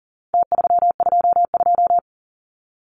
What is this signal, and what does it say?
Morse code, heard as a keyed tone, about 40 wpm, 700 Hz. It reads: T322
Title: Numbers - Contest Style - 4-Digit 40wpm